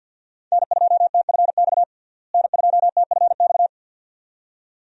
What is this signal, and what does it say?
Morse code, heard as a keyed tone, about 50 wpm, 700 Hz. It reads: N2TFX N2TFX
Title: USA Call Signs - 5 Characters - Repeated 2x - 50wpm